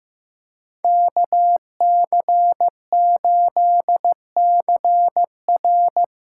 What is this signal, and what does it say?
Morse code, heard as a keyed tone, about 15 wpm, 700 Hz. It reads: KC8CR
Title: USA Call Signs - 8x Character Spacing - 15wpm